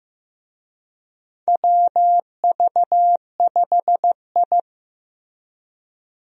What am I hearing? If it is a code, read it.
Morse code, heard as a keyed tone, about 15 wpm, 700 Hz. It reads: WV5I
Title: USA Call Signs - 4 Characters - Repeated 2x - 15wpm